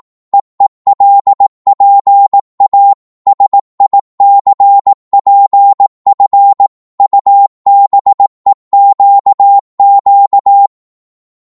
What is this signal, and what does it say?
Morse code, heard as a keyed tone, about 18 wpm, 800 Hz. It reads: EELPA SICPF UBEQQ
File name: Morse / ltr-18wpm